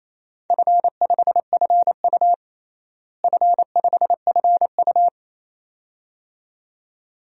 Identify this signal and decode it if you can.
Morse code, heard as a keyed tone, about 28 wpm, 700 Hz. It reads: F5FU F5FU
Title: DXCC-100 Call Signs - 4 Characters - Repeated 2x - 28wpm